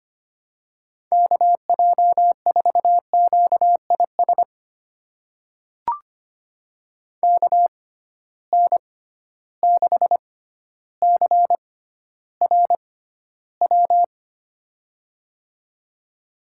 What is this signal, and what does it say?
Morse code, heard as a keyed tone, about 25 wpm, spaced as for 13 wpm, 700 Hz. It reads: KJ4QIS KN6CRW
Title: USA Call Signs - 6 Characters - 6x Character Spacing - 25wpm